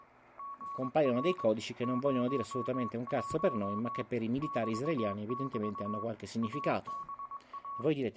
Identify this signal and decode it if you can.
Morse code, heard as a keyed tone, about 20 wpm, 1120 Hz, with speech. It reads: GR 01 <BT><BT> ABA